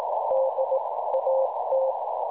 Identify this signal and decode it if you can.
Morse code, heard as a keyed tone, about 18 wpm, spaced as for 15 wpm, 530 Hz, with noise. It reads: DAT
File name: cwronde / websdr recording .6kHz